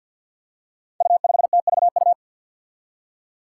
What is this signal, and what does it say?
Morse code, heard as a keyed tone, about 49 wpm, 700 Hz. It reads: U5TVU
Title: Five Challenging Letter-Number Combinations 50wpm